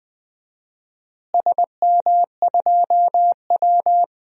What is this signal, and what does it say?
Morse code, heard as a keyed tone, about 20 wpm, 700 Hz. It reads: SM2W
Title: DXCC-100 Call Signs - 4 Characters - Repeated 3x - 20wpm